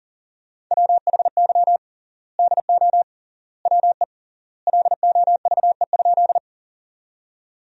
Morse code, heard as a keyed tone, about 40 wpm, 700 Hz. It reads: WHY DO WE LOVE?
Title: 3x Word-Spacing - Sentences from Top 400 Words - 40wpm